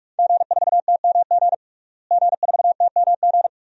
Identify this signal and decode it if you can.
Morse code, heard as a keyed tone, about 46 wpm, 700 Hz. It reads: G4TMG G4TMG
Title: DXCC-100 Call Signs - 5 Characters - Repeated 2x - 45wpm